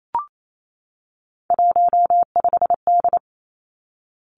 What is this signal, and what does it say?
Morse code, heard as a keyed tone, about 28 wpm, 700 Hz. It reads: 15D
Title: Three Challenging Letter-Number Combinations 28wpm